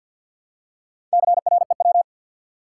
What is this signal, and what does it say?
Morse code, heard as a keyed tone, about 52 wpm, 700 Hz. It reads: CREW
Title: Four Letter Words - 50wpm